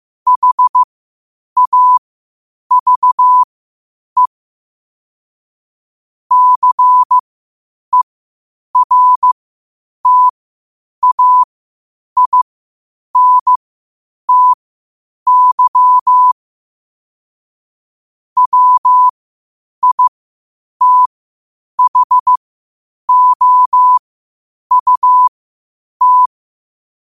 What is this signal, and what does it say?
Morse code, heard as a keyed tone, about 15 wpm, spaced as for 8 wpm, 1000 Hz. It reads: HAVE CERTAINTY WITHOUT